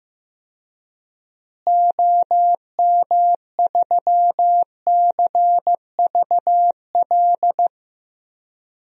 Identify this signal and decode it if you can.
Morse code, heard as a keyed tone, about 15 wpm, 700 Hz. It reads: OM3CVL
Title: DXCC-100 Call Signs - 6 Characters - Repeated 3x - 15wpm